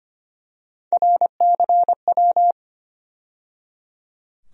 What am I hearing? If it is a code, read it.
Morse code, heard as a keyed tone, about 25 wpm, 700 Hz. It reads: RCW
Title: USA Call Sign Postfixes - 3 Letters 25wpm